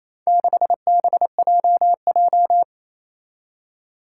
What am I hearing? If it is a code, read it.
Morse code, heard as a keyed tone, about 28 wpm, 700 Hz. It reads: 6BJJ
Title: Four Challenging Letter-Number Combinations 28wpm